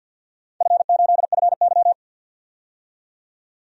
Morse code, heard as a keyed tone, about 51 wpm, 700 Hz. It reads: F8FY
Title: DXCC-100 Call Signs - 4 Characters - 50wpm